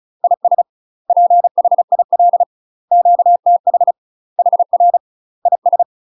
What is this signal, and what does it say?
Morse code, heard as a keyed tone, about 36 wpm, 700 Hz. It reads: IS PHIL QTH HR IS
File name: UK MASTER QSOs TRAINING / 35WPM / UK MASTER QSOs 0009